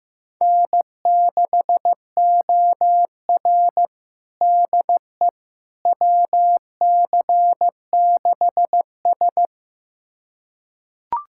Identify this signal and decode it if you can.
Morse code, heard as a keyed tone, about 15 wpm, 700 Hz. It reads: N6OR DE WC6S
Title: USA Callsign DE Callsign - 4 Characters - 15wpm